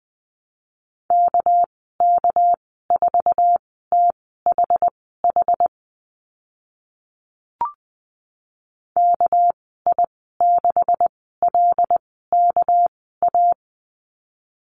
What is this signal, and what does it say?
Morse code, heard as a keyed tone, about 20 wpm, spaced as for 15 wpm, 700 Hz. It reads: KK4THH KI6LKA
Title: 200 Lesson 32g - Letter k - Call Signs 20wpm